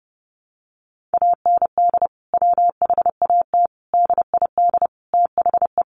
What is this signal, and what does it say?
Morse code, heard as a keyed tone, about 30 wpm, 700 Hz. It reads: AND WHAT DID THE
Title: Sentences from Top 1500 Words 30wpm